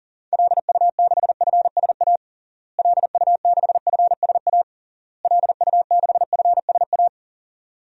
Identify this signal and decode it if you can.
Morse code, heard as a keyed tone, about 41 wpm, 700 Hz. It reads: LU6FSA LU6FSA LU6FSA
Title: DXCC-100 Call Signs - 6 Characters - Repeated 3x - 40wpm